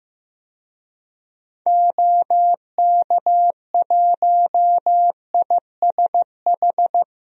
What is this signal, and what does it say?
Morse code, heard as a keyed tone, about 15 wpm, 700 Hz. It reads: OK1ISH
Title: DXCC-100 Call Signs - 6 Characters - 15wpm